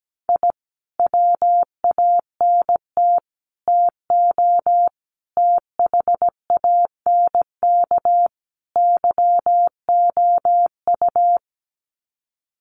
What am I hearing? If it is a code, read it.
Morse code, heard as a keyed tone, about 17 wpm, 700 Hz. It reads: I WANT TO THANK YOU
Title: Sets of 5 Words Encore 17wpm